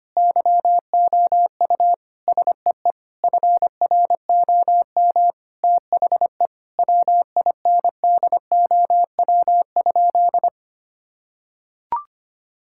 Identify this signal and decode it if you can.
Morse code, heard as a keyed tone, about 25 wpm, 700 Hz. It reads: YOU SEE FROM THE WINDOW?
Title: Sentences from Top 1600 Words 25wpm